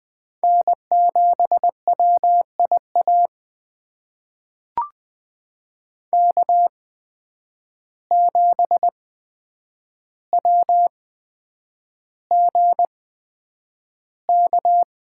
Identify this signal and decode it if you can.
Morse code, heard as a keyed tone, about 20 wpm, spaced as for 9 wpm, 700 Hz. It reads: N7WIA K7WGK
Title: USA Call Signs - 5 Characters - 8x Character Spacing - 20wpm